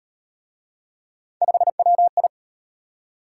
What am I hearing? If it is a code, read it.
Morse code, heard as a keyed tone, about 35 wpm, 700 Hz. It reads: 5WI